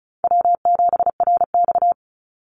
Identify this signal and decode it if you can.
Morse code, heard as a keyed tone, about 34 wpm, 700 Hz. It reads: W7RX
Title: Call Signs - 35wpm